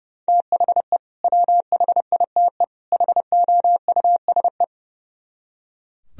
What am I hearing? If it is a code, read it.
Morse code, heard as a keyed tone, about 36 wpm, 700 Hz. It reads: THE WHITE HOUSE
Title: Speed-Racing - Sets of 3 Words 30wpm